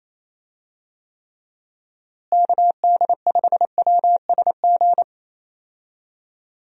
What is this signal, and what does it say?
Morse code, heard as a keyed tone, about 28 wpm, 700 Hz. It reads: KD5WSG
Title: USA Call Signs - 6 Characters - Repeated 3x - 28wpm